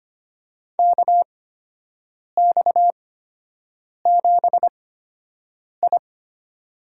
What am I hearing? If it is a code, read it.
Morse code, heard as a keyed tone, about 25 wpm, spaced as for 7 wpm, 700 Hz. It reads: KX7I
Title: USA Call Signs - 4 Characters - 8x Character Spacing - 25wpm